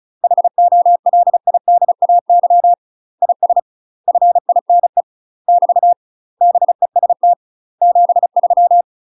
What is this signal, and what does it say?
Morse code, heard as a keyed tone, about 34 wpm, 700 Hz. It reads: HOLIDAY IS FINE <BT> BEST 73
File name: UK MASTER QSOs TRAINING / 35WPM / UK MASTER QSOs 0011